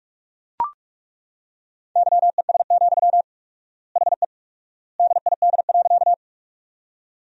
Morse code, heard as a keyed tone, about 41 wpm, 700 Hz. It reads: YES, HE DID.
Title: 4x Word-Spacing - Sentences from Top 500 Words - 45wpm